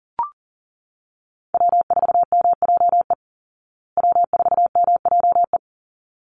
Morse code, heard as a keyed tone, about 40 wpm, 700 Hz. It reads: W4MJE W4MJE
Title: Call Signs Repeated 3x - 40wpm